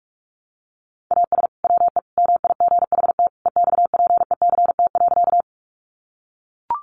Morse code, heard as a keyed tone, about 44 wpm, 700 Hz. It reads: AS WE MIGHT EXPECT.
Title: Sentences from Top 1100 Words 45wpm